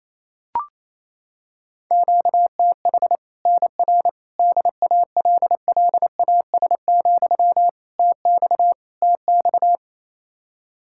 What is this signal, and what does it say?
Morse code, heard as a keyed tone, about 28 wpm, 700 Hz. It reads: QTH NR DALLAS, TX TX
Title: Practice QSO - Exchange 1 - Beginning - 28wpm